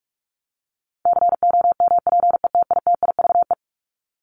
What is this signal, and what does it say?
Morse code, heard as a keyed tone, about 44 wpm, 700 Hz. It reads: COMPETITIVE
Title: Top 1001-1250 Words 45wpm